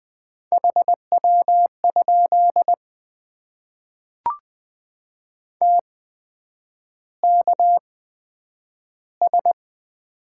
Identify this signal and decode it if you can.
Morse code, heard as a keyed tone, about 20 wpm, spaced as for 9 wpm, 700 Hz. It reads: HW? TKS
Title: Top 90 QSO Elements - 8x Character Spacing - 20wpm